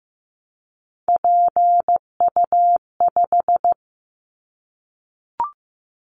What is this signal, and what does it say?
Morse code, heard as a keyed tone, about 15 wpm, 700 Hz. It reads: PU5
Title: Callsign Prefixes for Top 100 DXCC - 15wpm